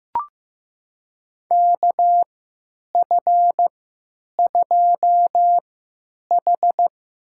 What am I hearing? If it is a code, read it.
Morse code, heard as a keyed tone, about 15 wpm, spaced as for 8 wpm, 700 Hz. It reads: KF2H
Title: USA Call Signs - 4 Characters - 3x Character Spacing - 15wpm